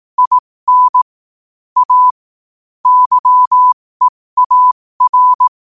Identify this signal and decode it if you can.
Morse code, heard as a keyed tone, about 18 wpm, spaced as for 15 wpm, 1000 Hz. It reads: IN A YEAR
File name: qotdfiles / 15wpm